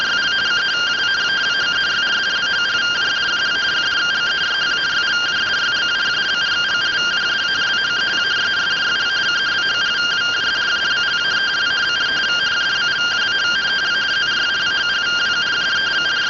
Two-tone teleprinter signal, about 170 Hz shift.